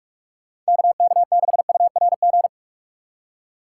Morse code, heard as a keyed tone, about 56 wpm, 700 Hz. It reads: KK6URG
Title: Speed Racing - USA 2x3 Call Signs 45wpm